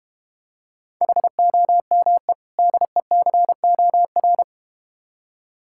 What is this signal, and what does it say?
Morse code, heard as a keyed tone, about 30 wpm, 700 Hz. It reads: HOME DECOR
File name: practice / 25wpm / Difficult Words - Speed-Racing with Hints - 25wpm